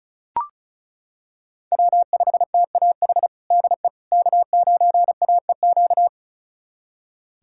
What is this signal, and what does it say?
Morse code, heard as a keyed tone, about 36 wpm, 700 Hz. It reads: W5TAH DE K9AEQ
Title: USA Callsign DE Callsign - 5 Characters - 35wpm